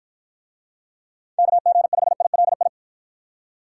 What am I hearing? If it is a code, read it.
Morse code, heard as a keyed tone, about 50 wpm, 700 Hz. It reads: KG5ILI